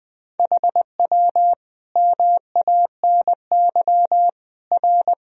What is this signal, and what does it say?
Morse code, heard as a keyed tone, about 20 wpm, 700 Hz. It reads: HW MANY R
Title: Sentences from Top 300 Words and Abbreviations 20wpm